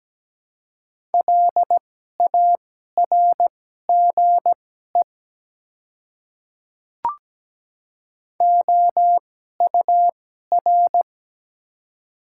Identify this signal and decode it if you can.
Morse code, heard as a keyed tone, about 17 wpm, spaced as for 12 wpm, 700 Hz. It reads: LARGE OUR
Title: 2x Character Spacing - Top 200 Words - 17wpm